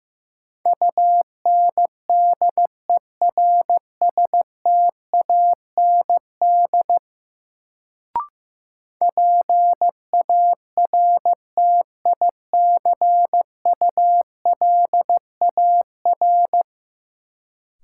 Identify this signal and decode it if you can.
Morse code, heard as a keyed tone, about 15 wpm, 700 Hz. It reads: UNDERSTAND PARTICULAR
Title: Sentences from Top 600 Words - New Word Drill - 15wpm